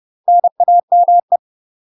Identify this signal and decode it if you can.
Morse code, heard as a keyed tone, about 30 wpm, 700 Hz. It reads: NAME
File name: UK MASTER QSOs TRAINING / 30WPM / UK MASTER QSOs 0004